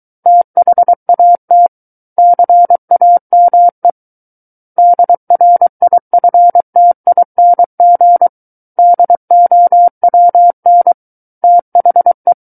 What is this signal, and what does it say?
Morse code, heard as a keyed tone, about 23 wpm, 700 Hz. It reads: THAT CAME DRIFTING DOWN THE
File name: Book One / 14-Chapter 14 - 23 WPM